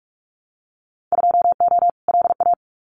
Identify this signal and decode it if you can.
Morse code, heard as a keyed tone, about 44 wpm, 700 Hz. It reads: 2O LA
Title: Winter Field Day Exchanges 45wpm